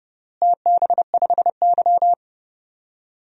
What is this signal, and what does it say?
Morse code, heard as a keyed tone, about 32 wpm, 700 Hz. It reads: TB5Y
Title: DXCC-100 Call Signs - 4 Characters - Repeated 3x - 30wpm